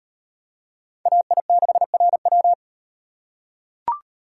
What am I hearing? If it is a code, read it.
Morse code, heard as a keyed tone, about 38 wpm, 700 Hz. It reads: AI6RW